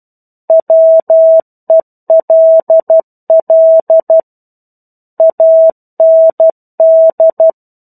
Morse code, heard as a keyed tone, about 12 wpm, 650 Hz. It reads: WELL AND
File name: 01-Chapter-01 12 WPM